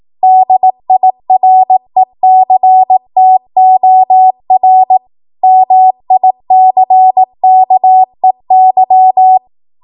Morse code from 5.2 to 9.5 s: MICKEY